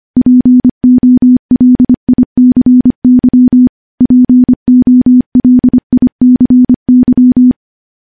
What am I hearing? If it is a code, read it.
Morse code, heard as a keyed tone, about 25 wpm, 250 Hz. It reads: POLICY POLICY
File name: morse-training-25WPM